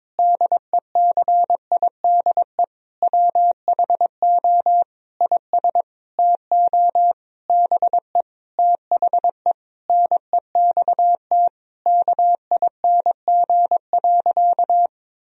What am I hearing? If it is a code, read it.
Morse code, heard as a keyed tone, about 22 wpm, 700 Hz. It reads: DECIDE WHO IS TO BE THE NEXT KING.
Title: Sentences from Top 1000 Words - Review Part 3 - 22wpm